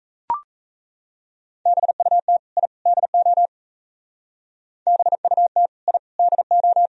Morse code, heard as a keyed tone, about 34 wpm, 700 Hz. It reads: BUT I DO BUT I DO
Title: Speed-Racing - Sets of 3 Words 28wpm